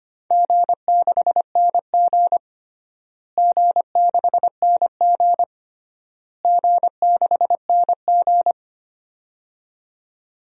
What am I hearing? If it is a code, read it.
Morse code, heard as a keyed tone, about 25 wpm, 700 Hz. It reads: G6NG G6NG G6NG
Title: DXCC-100 Call Signs - 4 Characters - Repeated 3x - 25wpm